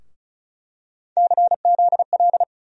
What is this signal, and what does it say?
Morse code, heard as a keyed tone, about 34 wpm, 700 Hz. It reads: CZL
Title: USA Call Sign Postfixes - 3 Letters 35wpm